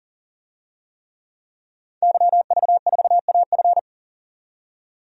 Morse code, heard as a keyed tone, about 40 wpm, 700 Hz. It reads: YV4AF
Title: DXCC-100 Call Signs - 5 Characters - 40wpm